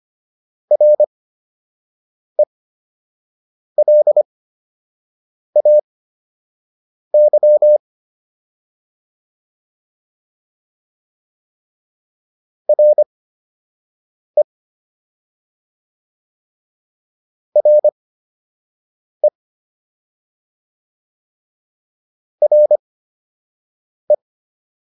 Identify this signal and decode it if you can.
Morse code, heard as a keyed tone, about 25 wpm, spaced as for 6 wpm, 600 Hz. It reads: RELAY RE RE RE